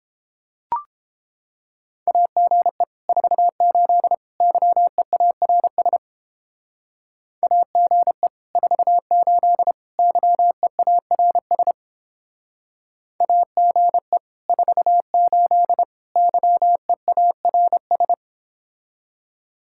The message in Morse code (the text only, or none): AGE 48 YEARS AGE 48 YEARS AGE 48 YEARS